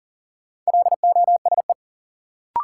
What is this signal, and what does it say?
Morse code, heard as a keyed tone, about 40 wpm, 700 Hz. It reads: LOSE
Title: Sentences from Top 1100 Words - New Word Drill Alt - 40wpm